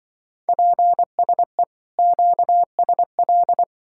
Morse code, heard as a keyed tone, about 23 wpm, 700 Hz. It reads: PSE QSL
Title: Speed-Racing Common QSO Phrases Encore - 20wpm